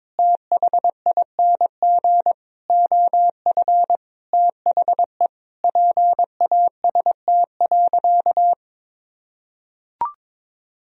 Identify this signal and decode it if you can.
Morse code, heard as a keyed tone, about 22 wpm, 700 Hz. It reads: THING OF THE PAST.
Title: Sentences from Top 1000 Words - Review Part 3 - 22wpm